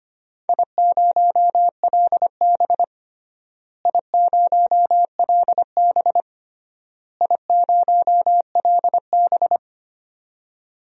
Morse code, heard as a keyed tone, about 25 wpm, 700 Hz. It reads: I0LB I0LB I0LB
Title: DXCC-100 Call Signs - 4 Characters - Repeated 3x - 25wpm